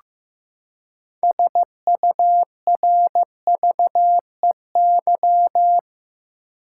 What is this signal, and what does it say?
Morse code, heard as a keyed tone, about 15 wpm, 700 Hz. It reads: SURVEY
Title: Six Letter Words - 15wpm